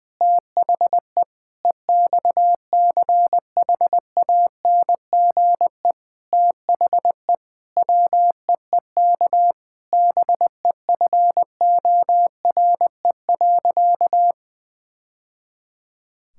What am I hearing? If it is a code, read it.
Morse code, heard as a keyed tone, about 20 wpm, 700 Hz. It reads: THE EXCHANGE THE WEEK BEFORE.
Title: Sentences from Top 1000 Words - Review Part 6 - 20wpm